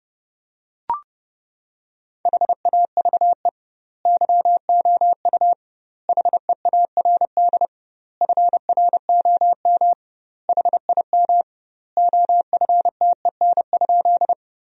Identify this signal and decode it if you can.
Morse code, heard as a keyed tone, about 30 wpm, 700 Hz. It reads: HAVE YOU HEARD FROM HIM OFTEN?
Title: 2x Word-Spacing - Sentences from Top 400 Words - 30wpm